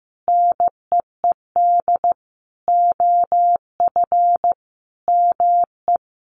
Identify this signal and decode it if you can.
Morse code, heard as a keyed tone, about 15 wpm, 700 Hz. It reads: NEED OF ME
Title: Sentences from Top 1400 Words 15wpm